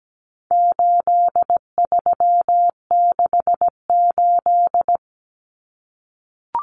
Morse code, heard as a keyed tone, about 17 wpm, 700 Hz. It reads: 8368
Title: Numbers - 4-Digit 17wpm